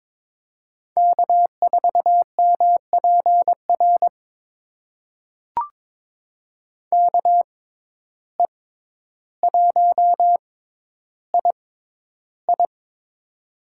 Morse code, heard as a keyed tone, about 22 wpm, spaced as for 11 wpm, 700 Hz. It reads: K4MPR KE1II
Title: USA Call Signs - 6x Character Spacing - 22wpm